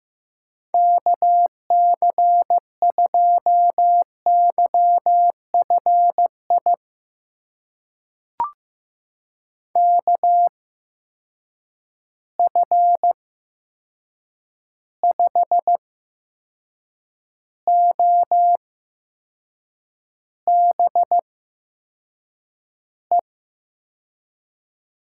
Morse code, heard as a keyed tone, about 15 wpm, spaced as for 6 wpm, 700 Hz. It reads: KC2YFI KF5OBE
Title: USA Call Signs - 6 Characters - 8x Character Spacing - 15wpm